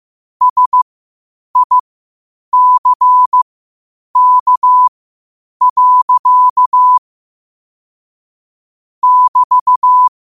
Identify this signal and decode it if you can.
Morse code, heard as a keyed tone, about 15 wpm, spaced as for 8 wpm, 1000 Hz. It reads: SICK. <BT>
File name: qotdfiles / 8wpm